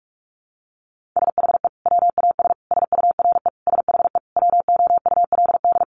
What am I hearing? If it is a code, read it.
Morse code, heard as a keyed tone, about 44 wpm, 700 Hz. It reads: SHE WAS SURE SHE WOULD
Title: Sentences from Top 1500 Words 45wpm